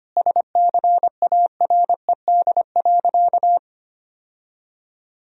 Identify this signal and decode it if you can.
Morse code, heard as a keyed tone, about 25 wpm, 700 Hz. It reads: SCARED.
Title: Sentences from Top 1000 Words - Review Part 4 - 25wpm